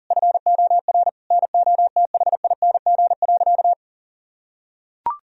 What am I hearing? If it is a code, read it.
Morse code, heard as a keyed tone, about 40 wpm, 700 Hz. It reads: FOR NOTHING.
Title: Sentences from Top 1000 Words - Review Part 3 - 40wpm